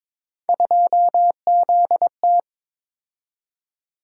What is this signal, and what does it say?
Morse code, heard as a keyed tone, about 21 wpm, 700 Hz. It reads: 2ZT
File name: practice / 22wpm / Three Letter-Number Combinations - 22wpm